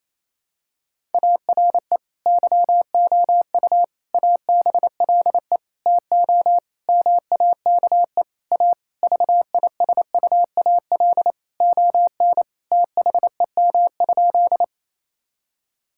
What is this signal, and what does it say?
Morse code, heard as a keyed tone, about 27 wpm, 700 Hz. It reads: ARE YOU ABLE TO MAKE A VISUAL ON THEM?
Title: Sentences from Top 1000 Words - Review Part 6 - 28wpm